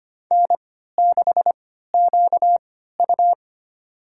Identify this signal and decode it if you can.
Morse code, heard as a keyed tone, about 25 wpm, spaced as for 14 wpm, 700 Hz. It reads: N6QU
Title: USA Call Signs - 4 Characters - 3x Character Spacing - 25wpm